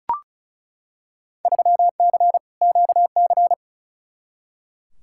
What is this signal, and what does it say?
Morse code, heard as a keyed tone, about 36 wpm, 690 Hz. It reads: 3C QC
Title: Field Day Exchanges Encore - 35wpm